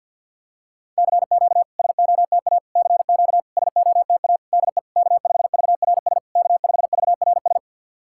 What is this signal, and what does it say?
Morse code, heard as a keyed tone, about 50 wpm, 690 Hz. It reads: CQ SOTA CQ SOTA DE K5VRS K5VRS
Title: SOTA QSO Practice 50wpm